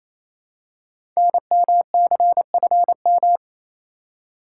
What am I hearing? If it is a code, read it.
Morse code, heard as a keyed tone, about 28 wpm, 700 Hz. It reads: NMCFM
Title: Code Groups - 5 Letters - 28wpm